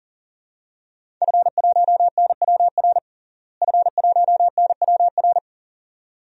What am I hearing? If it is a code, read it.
Morse code, heard as a keyed tone, about 40 wpm, 700 Hz. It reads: F1NWR F1NWR
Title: DXCC-100 Call Signs - 5 Characters - Repeated 2x - 40wpm